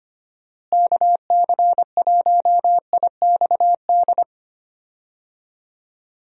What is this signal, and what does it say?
Morse code, heard as a keyed tone, about 25 wpm, 700 Hz. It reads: KC1IXD
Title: USA Call Signs - 6 Characters - Repeated 3x - 25wpm